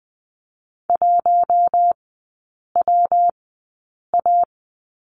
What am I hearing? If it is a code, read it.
Morse code, heard as a keyed tone, about 20 wpm, spaced as for 8 wpm, 700 Hz. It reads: 1WA
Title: 114 Lesson 19e - Letter w - Three Characters 20wpm